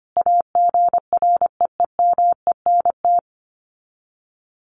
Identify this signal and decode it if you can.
Morse code, heard as a keyed tone, about 25 wpm, 700 Hz. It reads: AGREEMENT
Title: Sets of 5 Words Encore 25wpm